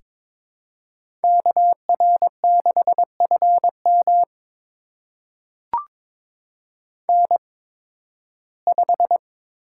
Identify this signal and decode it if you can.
Morse code, heard as a keyed tone, about 22 wpm, spaced as for 14 wpm, 700 Hz. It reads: KR6FM N5
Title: USA Call Signs - 5 Characters - 8x Character Spacing - 22wpm